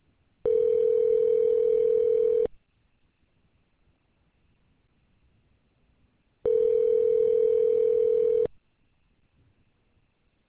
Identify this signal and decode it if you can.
Ringback tone (North America), 440 + 480 Hz.